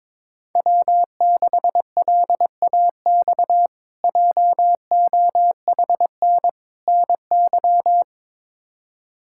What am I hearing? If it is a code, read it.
Morse code, heard as a keyed tone, about 22 wpm, 700 Hz. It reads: W6LAX JOHN NY
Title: K1USN SST QSO Practice - 22wpm